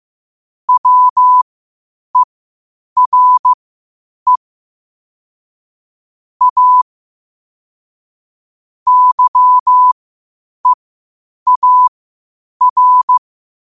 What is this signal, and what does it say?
Morse code, heard as a keyed tone, about 15 wpm, spaced as for 8 wpm, 1000 Hz. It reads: WERE A YEAR
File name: qotdfiles / 8wpm